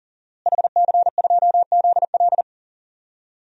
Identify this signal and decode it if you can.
Morse code, heard as a keyed tone, about 40 wpm, 690 Hz. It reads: HC2ZL
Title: DXCC-100 Call Signs - 5 Characters - Repeated 2x - 40wpm